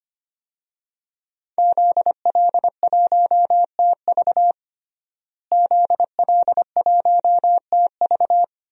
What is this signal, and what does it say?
Morse code, heard as a keyed tone, about 25 wpm, 700 Hz. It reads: ZL1TV ZL1TV
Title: DXCC-100 Call Signs - 5 Characters - Repeated 3x - 25wpm